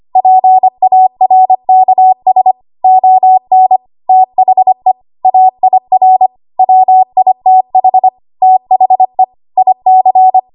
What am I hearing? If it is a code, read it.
Morse code, heard as a keyed tone, about 25 wpm, 750 Hz. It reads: PARKS ON THE AIR WITH THE IC
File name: Archive / 25 WPM / 25WPM